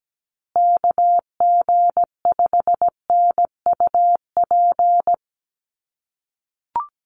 Morse code, heard as a keyed tone, about 17 wpm, 700 Hz. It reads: KG5NUP
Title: USA Call Signs Repeated 2x 17wpm